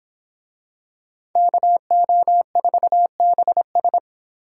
Morse code, heard as a keyed tone, about 26 wpm, 700 Hz. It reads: KO4BS